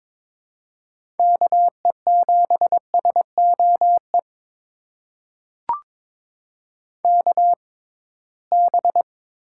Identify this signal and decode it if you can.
Morse code, heard as a keyed tone, about 22 wpm, spaced as for 17 wpm, 700 Hz. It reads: KE7SOE KB
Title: USA Call Signs - 6 Characters - 6x Character Spacing - 22wpm